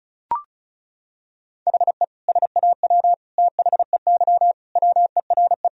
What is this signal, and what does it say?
Morse code, heard as a keyed tone, about 36 wpm, 700 Hz. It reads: HE SAW THEY WERE
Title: Sentences from Top 1000 Words - Review Part 4 - 35wpm